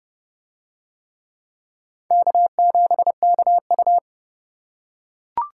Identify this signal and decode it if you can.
Morse code, heard as a keyed tone, about 27 wpm, 700 Hz. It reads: K7KU